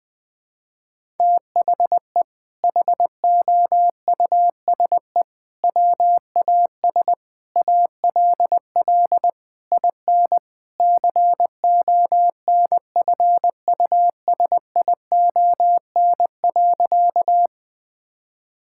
Morse code, heard as a keyed tone, about 20 wpm, 700 Hz. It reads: THE HOUSE WAS ALL IN CONFUSION.
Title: Sentences from Top 1600 Words 20wpm